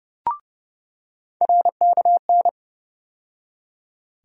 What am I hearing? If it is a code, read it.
Morse code, heard as a keyed tone, about 30 wpm, 700 Hz. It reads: RKN
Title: USA Call Sign Postfixes - 3 Letters 30wpm